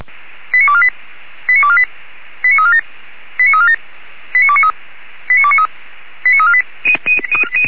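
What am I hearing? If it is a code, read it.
Five-tone selective call (ZVEI), heard as quick runs of five tones, tones of about 70 ms. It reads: 89248 89248 89347 89347 89293 89293 89348
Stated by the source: ZVEI five-tone selective call received on an SDR.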